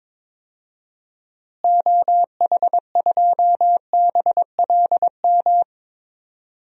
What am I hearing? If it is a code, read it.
Morse code, heard as a keyed tone, about 22 wpm, 700 Hz. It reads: OH2BLM
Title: DXCC-100 Call Signs - 6 Characters - Repeated 2x - 22wpm